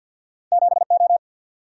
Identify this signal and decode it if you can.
Morse code, heard as a keyed tone, about 47 wpm, 700 Hz. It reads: 7O
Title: Single-Letter Single-Number 50wpm